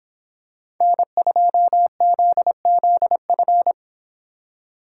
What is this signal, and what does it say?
Morse code, heard as a keyed tone, about 26 wpm, 700 Hz. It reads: N2ZZF